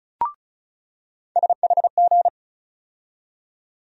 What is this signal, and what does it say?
Morse code, heard as a keyed tone, about 34 wpm, 700 Hz. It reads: SHG
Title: USA Call Sign Postfixes - 3 Letters 35wpm